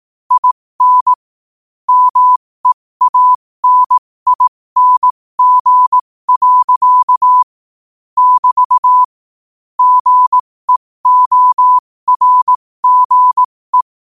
Morse code, heard as a keyed tone, about 18 wpm, spaced as for 15 wpm, 1000 Hz. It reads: IN MEANING. <BT> GEORGE
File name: qotdfiles / 15wpm